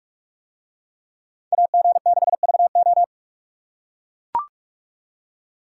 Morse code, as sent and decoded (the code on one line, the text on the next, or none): .- --. -.... ...- ---
AG6VO